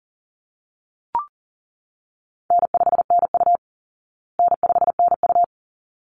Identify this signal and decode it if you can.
Morse code, heard as a keyed tone, about 40 wpm, 700 Hz. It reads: N5NU N5NU
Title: Call Signs Repeated 3x - 40wpm